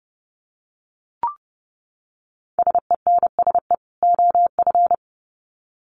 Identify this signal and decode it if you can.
Morse code, heard as a keyed tone, about 30 wpm, 700 Hz. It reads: SENSE OF
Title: Sets of 2 Words 30wpm